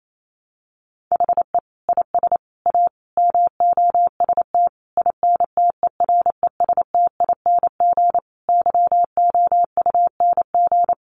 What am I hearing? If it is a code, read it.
Morse code, heard as a keyed tone, about 28 wpm, 700 Hz. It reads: HE IS A MOST INTERESTING YOUNG
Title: Sentences from Top 1000 Words 28wpm